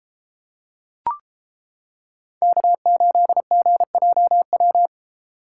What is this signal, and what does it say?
Morse code, heard as a keyed tone, about 29 wpm, 700 Hz. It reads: K8GJW